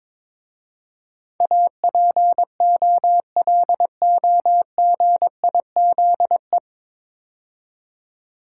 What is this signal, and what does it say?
Morse code, heard as a keyed tone, about 22 wpm, 690 Hz. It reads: APOLOGIZE
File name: 9 Letter Words - First 3 Letters Given - 22wpm